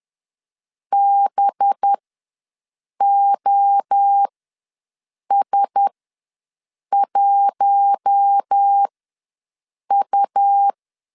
Morse code, heard as a keyed tone, about 10 wpm, spaced as for 6 wpm, 800 Hz. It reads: BOS1U
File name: LEZIONI / 5 LBVC / 5 LBVC ws 10wpm 3733